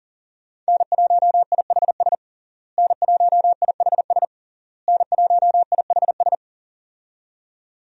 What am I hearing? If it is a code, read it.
Morse code, heard as a keyed tone, about 40 wpm, 700 Hz. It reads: N1IHS N1IHS N1IHS
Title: USA Call Signs - 5 Characters - Repeated 3x - 40wpm